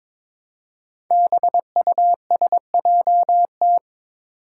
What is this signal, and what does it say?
Morse code, heard as a keyed tone, about 22 wpm, 700 Hz. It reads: BUSJT
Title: Five Challenging Letter-Number Combinations 22wpm